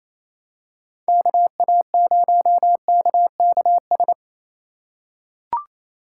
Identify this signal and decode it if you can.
Morse code, heard as a keyed tone, about 28 wpm, 700 Hz. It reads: KA0KKS
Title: USA Call Signs - 6 Characters - 2x Character Spacing - 28wpm